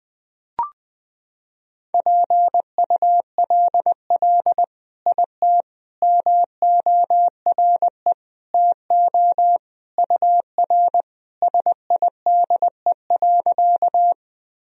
Morse code, heard as a keyed tone, about 20 wpm, 700 Hz. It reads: PULL IT MORE TO UR SIDE.
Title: Sentences from Top 400 Words and Abbreviations 20wpm